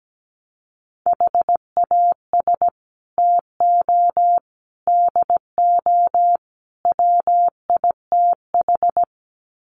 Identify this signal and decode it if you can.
Morse code, heard as a keyed tone, about 17 wpm, 700 Hz. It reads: HAS TO DO WITH
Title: Sets of 4 Words 17wpm